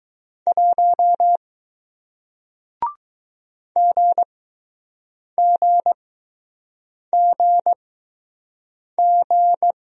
Morse code, heard as a keyed tone, about 20 wpm, spaced as for 6 wpm, 700 Hz. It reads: 1 GGGG